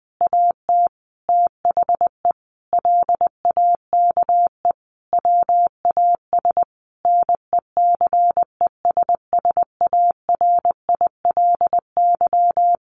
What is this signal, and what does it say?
Morse code, heard as a keyed tone, about 20 wpm, 700 Hz. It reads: AT THE LAKE WAS NECESSARILY
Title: Sentences from Top 1400 Words 20wpm